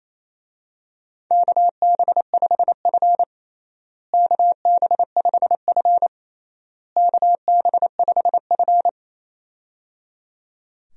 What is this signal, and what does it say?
Morse code, heard as a keyed tone, about 28 wpm, 700 Hz. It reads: KB5F KB5F KB5F
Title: USA Call Signs - 4 Characters - Repeated 3x - 28wpm